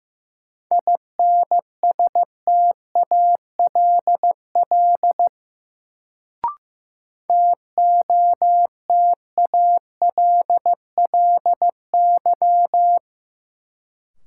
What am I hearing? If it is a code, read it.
Morse code, heard as a keyed tone, about 15 wpm, 700 Hz. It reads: INSTALL TOTALLY
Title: Sentences from Top 1600 Words - New Word Drill - 15wpm